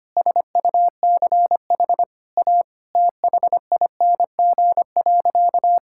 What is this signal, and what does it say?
Morse code, heard as a keyed tone, about 25 wpm, 700 Hz. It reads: SUCH A THING.
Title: Sentences from Top 1600 Words 25wpm